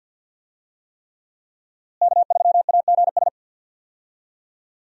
Morse code, heard as a keyed tone, about 51 wpm, 700 Hz. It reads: K3AGS
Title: USA Call Signs - 5 Characters - Repeated 3x - 50wpm